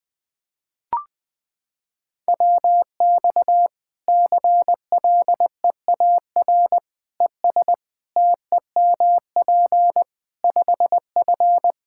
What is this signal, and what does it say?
Morse code, heard as a keyed tone, about 20 wpm, 700 Hz. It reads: WX CLEAR ES TEMP 5F
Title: Practice QSO - Exchange 2 - Info - 20wpm